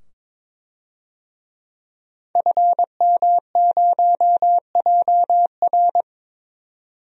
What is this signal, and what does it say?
Morse code, heard as a keyed tone, about 22 wpm, 700 Hz. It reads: FM0JR
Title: DXCC-100 Call Signs - 5 Characters - 22wpm